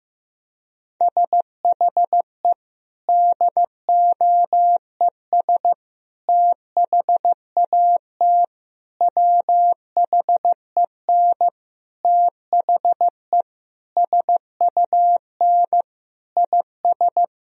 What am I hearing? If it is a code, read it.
Morse code, heard as a keyed tone, about 15 wpm, 700 Hz. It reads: SHE DOES THAT WHEN THE SUN IS
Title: Sentences from Top 1000 Words - Review Part 1 - 15wpm